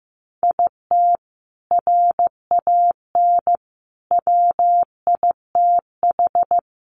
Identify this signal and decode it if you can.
Morse code, heard as a keyed tone, about 15 wpm, 700 Hz. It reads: IT RAN WITH
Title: A Sound Of Thunder 15wpm